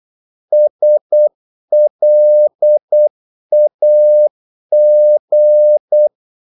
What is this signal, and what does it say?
Morse code, heard as a keyed tone, about 8 wpm, 600 Hz. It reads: SLAG
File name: Korte ord DK 8wpm